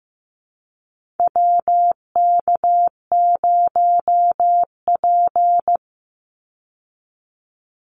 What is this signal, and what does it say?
Morse code, heard as a keyed tone, about 15 wpm, 700 Hz. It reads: WK0P
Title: Call Signs Repeated 2x - 15wpm